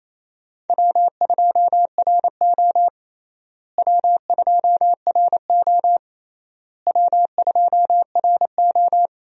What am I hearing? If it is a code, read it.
Morse code, heard as a keyed tone, about 28 wpm, 700 Hz. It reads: W2RO W2RO W2RO
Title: USA Call Signs - 4 Characters - Repeated 3x - 28wpm